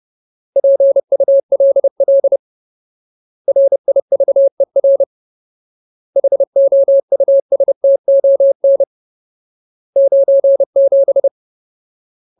Morse code, heard as a keyed tone, about 30 wpm, 550 Hz. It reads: PULL RIVER HOUSTON 97
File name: Ultimate Word List / 30WPM Ultimate